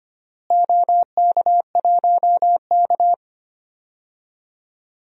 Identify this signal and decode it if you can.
Morse code, heard as a keyed tone, about 25 wpm, 700 Hz. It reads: OK1K
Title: DXCC-100 Call Signs - 4 Characters - Repeated 3x - 25wpm